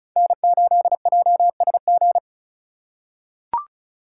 Morse code, heard as a keyed tone, about 34 wpm, 700 Hz. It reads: N8JSG
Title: USA Call Signs - 5 Characters - 3x Character Spacing - 35wpm